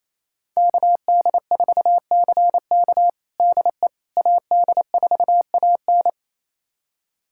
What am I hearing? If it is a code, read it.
Morse code, heard as a keyed tone, about 28 wpm, 700 Hz. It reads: KD4CK DE AD4AN
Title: USA Callsign DE Callsign - 5 Characters - 28wpm